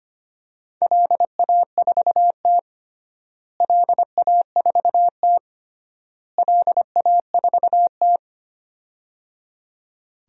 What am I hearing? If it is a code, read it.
Morse code, heard as a keyed tone, about 25 wpm, 700 Hz. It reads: LA4T LA4T LA4T
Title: DXCC-100 Call Signs - 4 Characters - Repeated 3x - 25wpm